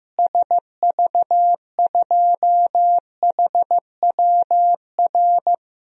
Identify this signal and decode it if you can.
Morse code, heard as a keyed tone, about 15 wpm, 700 Hz. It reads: SV2HWR
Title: DXCC-100 Call Signs - 6 Characters - Repeated 2x - 15wpm